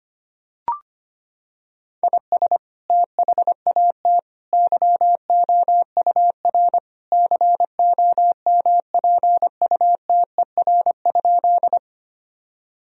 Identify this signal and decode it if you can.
Morse code, heard as a keyed tone, about 25 wpm, 700 Hz. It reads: IS THAT YOUR COMPUTER?
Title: Sentences from Top 1000 Words - Review Part 3 - 25wpm